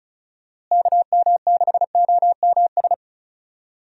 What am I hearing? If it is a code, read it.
Morse code, heard as a keyed tone, about 34 wpm, 700 Hz. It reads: KM6OMS
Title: USA Call Signs - 6 Characters - Repeated 2x - 35wpm